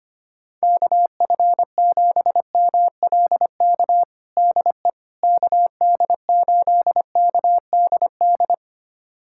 Morse code, heard as a keyed tone, about 25 wpm, 700 Hz. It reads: KF7MLK DE KD8KDD
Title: USA Callsign DE Callsign 25wpm